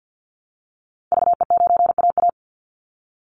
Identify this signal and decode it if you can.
Morse code, heard as a keyed tone, about 49 wpm, 700 Hz. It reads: VE9AA
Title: Call Signs Repeated 3x - 50wpm